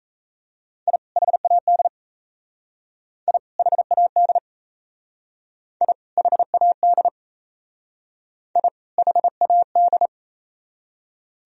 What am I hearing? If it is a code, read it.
Morse code, heard as a keyed tone, about 33 wpm, 700 Hz. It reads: I HAD I HAD I HAD I HAD
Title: Speed-Racing - Sets of 2 Words 28wpm